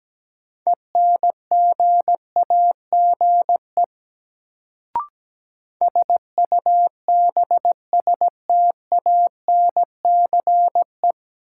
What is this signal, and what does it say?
Morse code, heard as a keyed tone, about 17 wpm, 700 Hz. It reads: ENGAGE SUBSTANCE
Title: Sentences from Top 1500 Words - New Word Drill - 17wpm